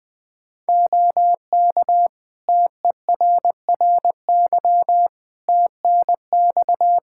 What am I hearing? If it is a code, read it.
Morse code, heard as a keyed tone, about 20 wpm, 700 Hz. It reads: OK TERRY TNX
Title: Practice QSO - Exchange 4 - Ending - 20wpm